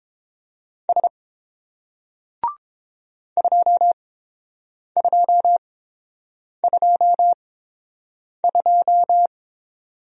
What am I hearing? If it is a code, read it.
Morse code, heard as a keyed tone, about 30 wpm, spaced as for 7 wpm, 700 Hz. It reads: S 2222